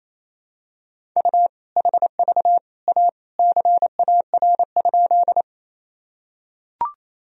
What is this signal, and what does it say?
Morse code, heard as a keyed tone, about 28 wpm, 700 Hz. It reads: U HV A CAR?
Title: Sentences from Top 300 Words and Abbreviations 28wpm